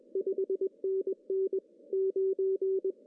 Morse code, heard as a keyed tone, about 24 wpm, 440 Hz. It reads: 5NN 9
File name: CW / cqww-cw-k5tr